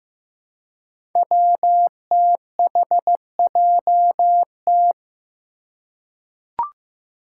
Morse code, heard as a keyed tone, about 15 wpm, 700 Hz. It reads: WTHJT
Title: Five Challenging Letter-Number Combinations 15wpm